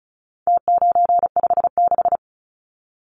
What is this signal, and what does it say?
Morse code, heard as a keyed tone, about 34 wpm, 700 Hz. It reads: T956
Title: Numbers - Contest Style - 4-Digit 35wpm